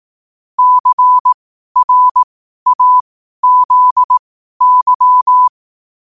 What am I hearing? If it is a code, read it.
Morse code, heard as a keyed tone, about 18 wpm, spaced as for 12 wpm, 1000 Hz. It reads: CRAZY
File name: qotdfiles / 12wpm